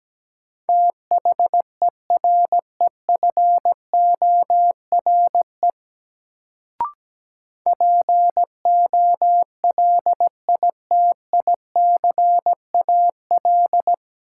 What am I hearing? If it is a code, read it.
Morse code, heard as a keyed tone, about 17 wpm, 700 Hz. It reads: THEREFORE POLITICAL
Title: Sentences from Top 800 Words - New Word Drill - 17wpm